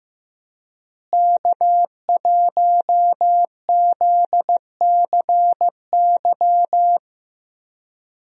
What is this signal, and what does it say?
Morse code, heard as a keyed tone, about 15 wpm, 700 Hz. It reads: K1ZCY
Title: USA Call Signs - 5 Characters - Repeated 3x - 15wpm